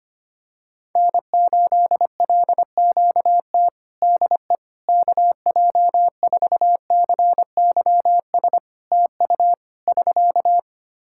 Morse code, heard as a keyed tone, about 25 wpm, 700 Hz. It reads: N8LQT DE KJ4CYS TU <SK>
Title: K1USN SST QSO Practice 25wpm